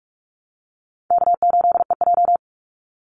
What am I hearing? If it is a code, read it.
Morse code, heard as a keyed tone, about 44 wpm, 700 Hz. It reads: K8EJ
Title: Call Signs Repeated 2x - 45wpm